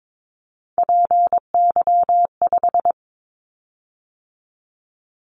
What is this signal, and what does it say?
Morse code, heard as a keyed tone, about 22 wpm, 700 Hz. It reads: PY5
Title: Callsign Prefixes for Top 50 DXCC - 22wpm